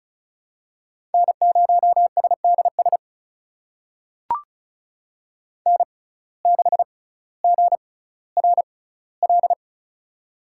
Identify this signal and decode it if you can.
Morse code, heard as a keyed tone, about 36 wpm, spaced as for 18 wpm, 700 Hz. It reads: N0SDS N6GRL
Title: USA Call Signs - 6x Character Spacing - 35wpm